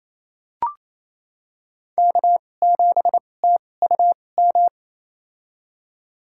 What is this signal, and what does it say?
Morse code, heard as a keyed tone, about 28 wpm, spaced as for 20 wpm, 700 Hz. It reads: K7TUM
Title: USA Call Signs - 5 Characters - 2x Character Spacing - 28wpm